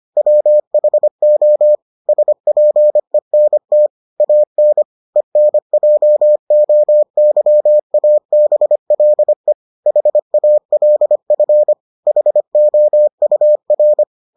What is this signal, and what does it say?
Morse code, heard as a keyed tone, about 25 wpm, 600 Hz. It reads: WHO SPENT AN ENJOYABLE HALF HOUR